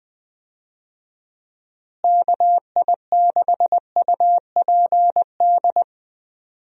Morse code, heard as a keyed tone, about 20 wpm, 700 Hz. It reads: KI6UPD
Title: USA Call Signs - 6 Characters - Repeated 3x - 20wpm